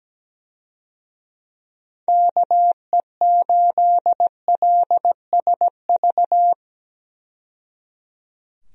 Morse code, heard as a keyed tone, about 17 wpm, 700 Hz. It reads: KE8LSV
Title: USA Call Signs - 6 Characters - 17wpm